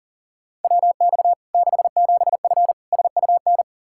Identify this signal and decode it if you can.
Morse code, heard as a keyed tone, about 40 wpm, 700 Hz. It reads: WX 67F SUN
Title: K1USN SST QSO Practice 40wpm